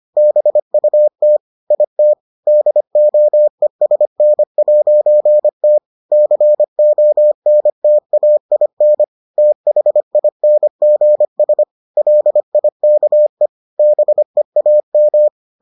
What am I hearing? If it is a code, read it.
Morse code, heard as a keyed tone, about 25 wpm, 600 Hz. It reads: BUT IT DOESN'T CONTAIN THINGS LIKE BEAM